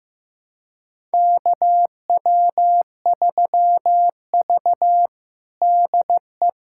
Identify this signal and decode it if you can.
Morse code, heard as a keyed tone, about 15 wpm, 700 Hz. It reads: KW3V DE
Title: USA Callsign DE Callsign - 4 Characters - 15wpm